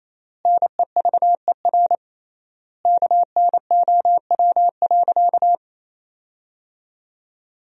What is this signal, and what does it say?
Morse code, heard as a keyed tone, about 28 wpm, 700 Hz. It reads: NEVER KNOW.
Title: 3x Word-Spacing - Sentences from Top 300 Words - 28wpm